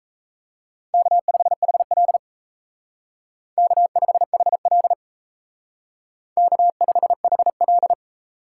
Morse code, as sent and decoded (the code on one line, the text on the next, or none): -.- ..... .... .-.. / -.- ..... .... .-.. / -.- ..... .... .-..
K5HL K5HL K5HL